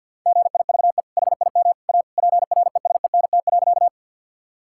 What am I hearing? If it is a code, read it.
Morse code, heard as a keyed tone, about 52 wpm, 700 Hz. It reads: GIVE HIM A PRESENT.
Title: Sentences from Top 1000 Words - Review Part 5 - 50wpm